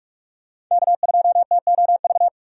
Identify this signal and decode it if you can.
Morse code, heard as a keyed tone, about 46 wpm, 700 Hz. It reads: K2TOV
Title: USA Call Signs - 5 Characters - 45wpm